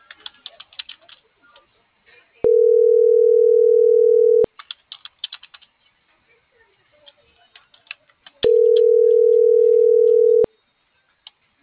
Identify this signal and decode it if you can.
Ringback tone (North America), 440 + 480 Hz.